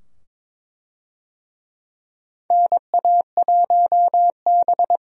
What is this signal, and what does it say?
Morse code, heard as a keyed tone, about 22 wpm, 700 Hz. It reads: NA1B
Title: USA Call Signs - 4 Characters - 22wpm